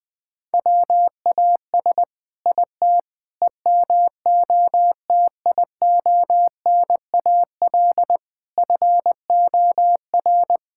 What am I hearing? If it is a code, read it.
Morse code, heard as a keyed tone, about 21 wpm, 700 Hz. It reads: WAS IT EMOTIONAL FOR
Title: Sentences from Top 1000 Words - Review Part 5 - 20wpm